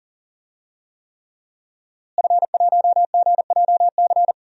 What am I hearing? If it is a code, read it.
Morse code, heard as a keyed tone, about 40 wpm, 700 Hz. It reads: F1GJC
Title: DXCC-100 Call Signs - 5 Characters - Repeated 3x - 40wpm